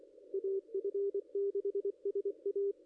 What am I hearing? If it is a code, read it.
Morse code, heard as a keyed tone, about 27 wpm, 410 Hz, with radio noise. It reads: AF6SA
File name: contests / ss-cw-k5tr-op-n5rz -0200